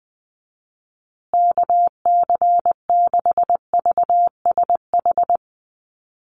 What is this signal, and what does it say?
Morse code, heard as a keyed tone, about 20 wpm, 700 Hz. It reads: KC6VSH
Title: USA Call Signs Repeated 2x 20wpm